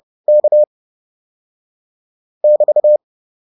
Morse code, heard as a keyed tone, about 30 wpm, 600 Hz. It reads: K <BT>